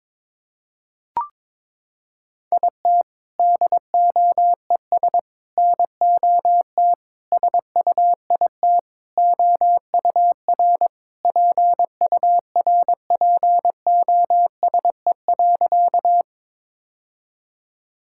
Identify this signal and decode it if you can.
Morse code, heard as a keyed tone, about 22 wpm, 700 Hz. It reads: IT DOES NOT SUIT OUR PURPOSE.
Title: Sentences from Top 1600 Words 22wpm